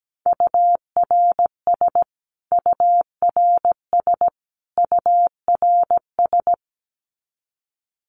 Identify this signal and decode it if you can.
Morse code, heard as a keyed tone, about 17 wpm, 700 Hz. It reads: URS URS URS
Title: Abbreviations Repeated 3x - 17wpm